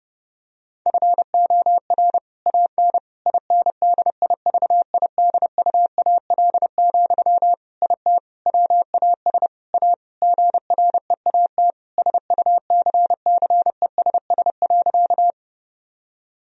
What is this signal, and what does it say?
Morse code, heard as a keyed tone, about 30 wpm, 700 Hz. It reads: FOR AN INDIVIDUAL, IT WAS A GREAT SUCCESS.
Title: Sentences from Top 1000 Words - Review Part 4 - 30wpm